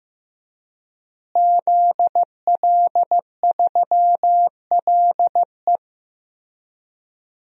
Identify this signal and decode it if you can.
Morse code, heard as a keyed tone, about 15 wpm, 700 Hz. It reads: ZL3LE
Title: DXCC-100 Call Signs - 5 Characters - Repeated 2x - 15wpm